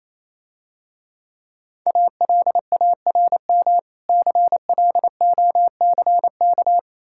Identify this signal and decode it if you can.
Morse code, heard as a keyed tone, about 28 wpm, 700 Hz. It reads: ALARM CLOCK